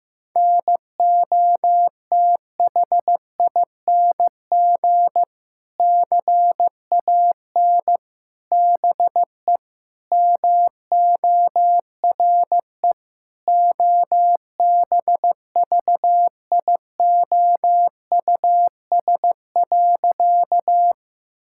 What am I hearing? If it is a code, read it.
Morse code, heard as a keyed tone, about 16 wpm, 700 Hz. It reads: NOTHING CAN BE MORE OBVIOUS.
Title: Sentences from Top 1600 Words 15wpm